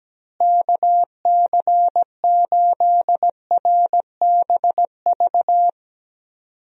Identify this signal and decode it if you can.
Morse code, heard as a keyed tone, about 17 wpm, 700 Hz. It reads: KC8RBV
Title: USA Call Signs - 6 Characters - Repeated 2x - 17wpm